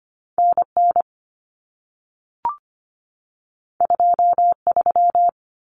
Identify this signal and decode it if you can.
Morse code, heard as a keyed tone, about 25 wpm, 700 Hz. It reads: NN 23
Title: Common Cut Numbers - 2-Digit 25wpm